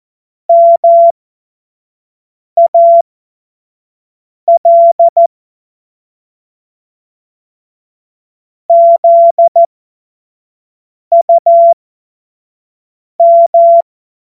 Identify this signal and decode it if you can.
Morse code, heard as a keyed tone, about 14 wpm, spaced as for 5 wpm, 680 Hz. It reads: MAL ZUM
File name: Morsekurs / Witze / Witz 21. März 2023 25 F70